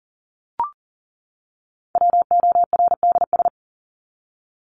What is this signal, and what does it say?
Morse code, heard as a keyed tone, about 40 wpm, 700 Hz. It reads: WORDS
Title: ES:S Suffix - Top 100 Words with Base Words 40wpm